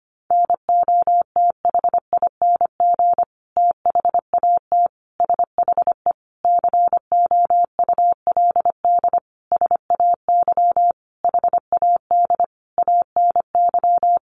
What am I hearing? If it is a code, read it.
Morse code, heard as a keyed tone, about 25 wpm, 700 Hz. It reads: NOTHING THAT SHE COULD SAY HAD ANY
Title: Sentences from Top 1100 Words 25wpm